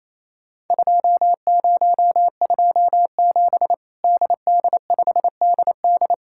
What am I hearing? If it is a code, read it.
Morse code, heard as a keyed tone, about 28 wpm, 700 Hz. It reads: 2027 DD5DD
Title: Worked All Europe DX Contest - QTC Exchange - 28wpm